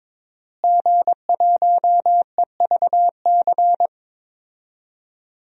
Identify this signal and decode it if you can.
Morse code, heard as a keyed tone, about 22 wpm, 700 Hz. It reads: G1EVC
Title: DXCC-100 Call Signs - 5 Characters - Repeated 3x - 22wpm